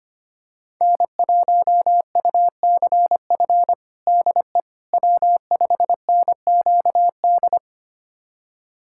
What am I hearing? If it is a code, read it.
Morse code, heard as a keyed tone, about 25 wpm, 700 Hz. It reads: N1UCF DE W5NQD
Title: USA Callsign DE Callsign - 5 Characters - 25wpm